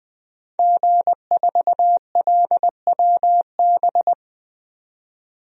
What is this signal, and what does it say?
Morse code, heard as a keyed tone, about 20 wpm, 700 Hz. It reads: G4LWB
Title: DXCC-100 Call Signs - 5 Characters - 20wpm